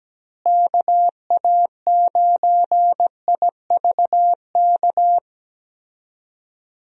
Morse code, heard as a keyed tone, about 17 wpm, 700 Hz. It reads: KA9IVK
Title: USA Call Signs - 6 Characters - Repeated 2x - 17wpm